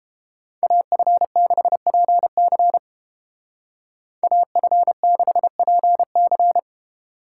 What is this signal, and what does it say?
Morse code, heard as a keyed tone, about 29 wpm, 700 Hz. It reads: AF6PC AF6PC